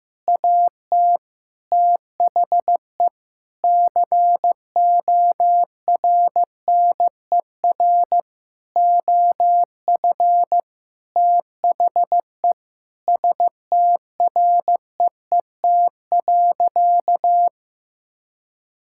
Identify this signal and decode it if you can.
Morse code, heard as a keyed tone, about 15 wpm, 700 Hz. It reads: AT THE CORNER OF THE STREET.
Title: Sentences from Top 1600 Words 15wpm